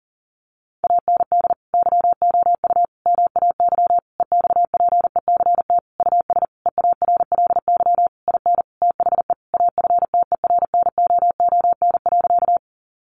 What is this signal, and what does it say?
Morse code, heard as a keyed tone, about 40 wpm, 700 Hz. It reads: AND YOU MAY EXPECT US EARLY IN THE AFTERNOON.
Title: Sentences from Top 1500 Words 40wpm